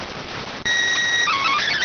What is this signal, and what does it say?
SSTV (slow-scan television), PD 290.